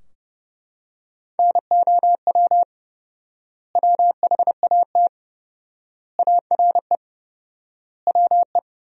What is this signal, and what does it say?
Morse code, heard as a keyed tone, about 30 wpm, 700 Hz. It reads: NOW WHAT ARE WE
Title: 4x Word-Spacing - Sentences from Top 100 Words - 30wpm